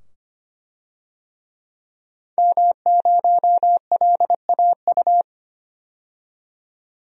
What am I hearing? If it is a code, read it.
Morse code, heard as a keyed tone, about 25 wpm, 700 Hz. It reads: M0LAU
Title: DXCC-100 Call Signs - 5 Characters - 25wpm